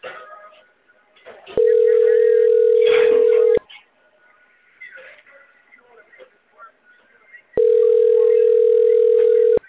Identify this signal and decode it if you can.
Ringback tone (North America), 440 + 480 Hz.